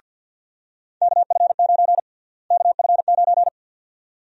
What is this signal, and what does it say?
Morse code, heard as a keyed tone, about 47 wpm, 700 Hz. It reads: KF9 KF9
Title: USA Call Sign Prefixes - Repeated 2x - 50wpm